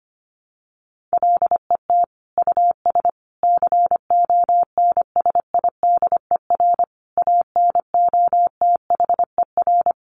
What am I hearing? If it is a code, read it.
Morse code, heard as a keyed tone, about 25 wpm, 700 Hz. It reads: LET US CONSIDER ANOTHER
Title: Sentences from Top 600 Words 25wpm